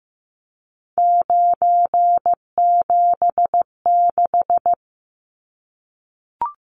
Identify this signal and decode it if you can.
Morse code, heard as a keyed tone, about 15 wpm, 700 Hz. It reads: 976
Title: Numbers - 3-Digit 15wpm